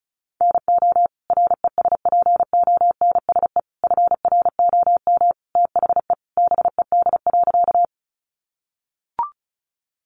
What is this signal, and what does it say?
Morse code, heard as a keyed tone, about 34 wpm, 700 Hz. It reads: NO RESPONSE FROM THE BED.
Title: Sentences from Top 1400 Words 35wpm